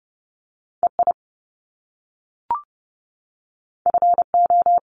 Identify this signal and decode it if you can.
Morse code, heard as a keyed tone, about 30 wpm, 700 Hz. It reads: EI FO
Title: Top Two Letter Combinations in Words 30wpm